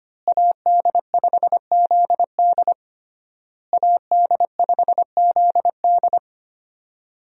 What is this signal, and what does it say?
Morse code, heard as a keyed tone, about 25 wpm, 700 Hz. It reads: AD5ZD AD5ZD
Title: USA Call Signs - 5 Characters - Repeated 3x - 25wpm